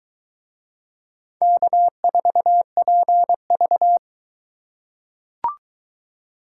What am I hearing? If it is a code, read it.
Morse code, heard as a keyed tone, about 20 wpm, 700 Hz. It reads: K4PV